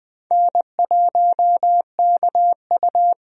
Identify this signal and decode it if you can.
Morse code, heard as a keyed tone, about 20 wpm, 700 Hz. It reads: N1KU
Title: USA Call Signs - 4 Characters - Repeated 2x - 20wpm